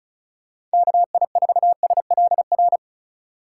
Morse code, heard as a keyed tone, about 36 wpm, 700 Hz. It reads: KI4SLR
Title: USA Call Signs - 6 Characters - 3x Character Spacing - 35wpm